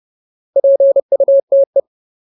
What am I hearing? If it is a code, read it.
Morse code, heard as a keyed tone, about 30 wpm, 550 Hz. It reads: PUTE
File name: Ultimate Word List / 30WPM Ultimate